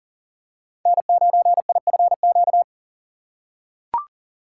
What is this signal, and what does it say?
Morse code, heard as a keyed tone, about 40 wpm, 700 Hz. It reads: N9IFQ
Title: USA Call Signs - 3x Character Spacing - 40wpm